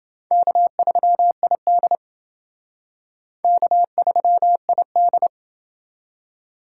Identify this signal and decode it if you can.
Morse code, heard as a keyed tone, about 29 wpm, 690 Hz. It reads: K3ID K3ID